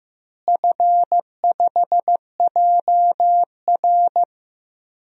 Morse code, heard as a keyed tone, about 16 wpm, 700 Hz. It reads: F5JR
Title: DXCC-100 Call Signs - 4 Characters - Repeated 3x - 15wpm